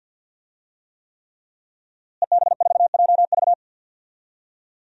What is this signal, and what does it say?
Morse code, heard as a keyed tone, about 47 wpm, 700 Hz. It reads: EB4JV
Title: DXCC-100 Call Signs - 5 Characters - Repeated 3x - 50wpm